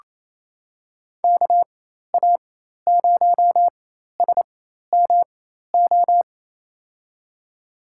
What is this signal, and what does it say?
Morse code, heard as a keyed tone, about 28 wpm, spaced as for 13 wpm, 700 Hz. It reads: KA0SMO
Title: USA Call Signs - 4x Character Spacing - 28wpm